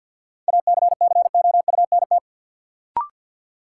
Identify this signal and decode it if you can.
Morse code, heard as a keyed tone, about 47 wpm, 700 Hz. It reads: ACCOUNT
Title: Sentences from Top 900 Words - New Word Drill - 50wpm